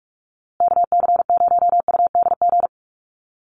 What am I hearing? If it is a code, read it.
Morse code, heard as a keyed tone, about 44 wpm, 690 Hz. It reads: KC0UDG
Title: USA Call Signs Repeated 3x 45wpm